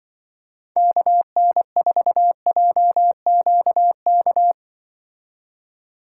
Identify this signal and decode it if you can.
Morse code, heard as a keyed tone, about 24 wpm, 700 Hz. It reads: KN4JQK
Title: Speed Racing - USA 2x3 Call Signs 20wpm